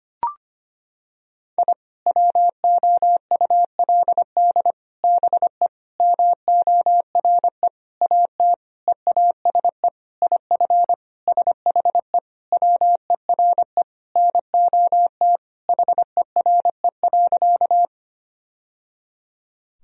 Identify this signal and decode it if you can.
Morse code, heard as a keyed tone, about 25 wpm, 700 Hz. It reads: I WOULD BE MORE AT EASE IF SHE WERE NOT HERE.
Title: Sentences from Top 1000 Words - Review Part 1 - 25wpm